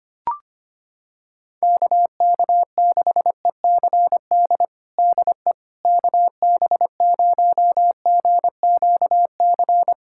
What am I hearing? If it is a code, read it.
Morse code, heard as a keyed tone, about 25 wpm, 700 Hz. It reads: KK6ECD DE KB0GQC
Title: USA Callsign DE Callsign - 6 Characters - 25wpm